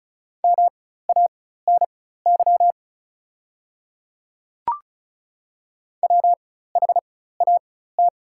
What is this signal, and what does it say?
Morse code, heard as a keyed tone, about 34 wpm, spaced as for 16 wpm, 700 Hz. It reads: MANY WHAT
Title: 4x Character Spacing - Top 100 Words - 35wpm